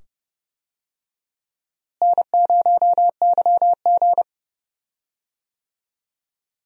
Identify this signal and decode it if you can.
Morse code, heard as a keyed tone, about 30 wpm, 700 Hz. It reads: N0YG
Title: USA Call Signs - 4 Characters - Repeated 2x - 30wpm